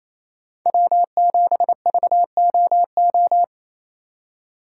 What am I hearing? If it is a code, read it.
Morse code, heard as a keyed tone, about 34 wpm, 700 Hz. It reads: W7VOO
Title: Speed Racing - USA 1x3 Call Signs 28wpm